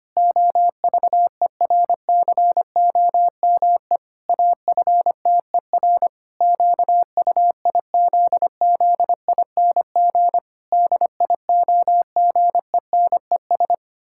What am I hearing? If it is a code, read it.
Morse code, heard as a keyed tone, about 25 wpm, 700 Hz. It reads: OVERCOME AFTER QUIZZING DIOGENES
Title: Pangram Challenge v2 - 25wpm